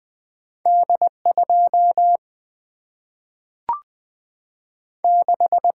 Morse code, heard as a keyed tone, about 20 wpm, 700 Hz. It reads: D2 6
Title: Single-Letter Single-Number 20wpm